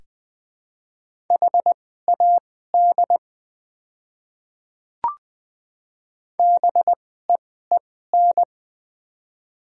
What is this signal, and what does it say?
Morse code, heard as a keyed tone, about 21 wpm, spaced as for 15 wpm, 700 Hz. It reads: HAD BEEN
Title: 2x Character Spacing - Top 100 Words - 20wpm